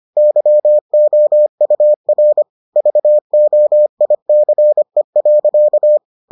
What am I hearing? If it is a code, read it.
Morse code, heard as a keyed tone, about 25 wpm, 600 Hz. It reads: YOUR VOICE.